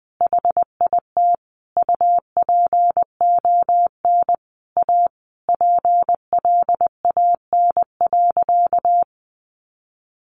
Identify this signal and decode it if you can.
Morse code, heard as a keyed tone, about 20 wpm, 700 Hz. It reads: HIT UPON A PLAN.
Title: Sentences from Top 1200 Words 20wpm